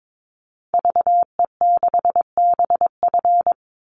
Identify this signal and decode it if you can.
Morse code, heard as a keyed tone, about 22 wpm, 700 Hz. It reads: VE6BF
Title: Call Signs Repeated 2x - 22wpm